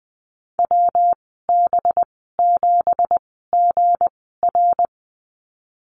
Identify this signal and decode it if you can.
Morse code, heard as a keyed tone, about 20 wpm, spaced as for 14 wpm, 700 Hz. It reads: WB7GR
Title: 245 Lesson 39g - Letter z - Call Signs 20wpm